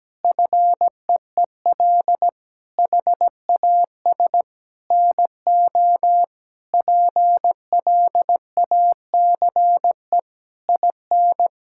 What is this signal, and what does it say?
Morse code, heard as a keyed tone, about 17 wpm, 700 Hz. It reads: FEEL HAS NO PLACE IN
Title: Sentences from Top 1600 Words 17wpm